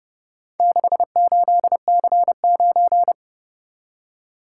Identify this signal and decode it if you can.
Morse code, heard as a keyed tone, about 29 wpm, 700 Hz. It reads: 68C9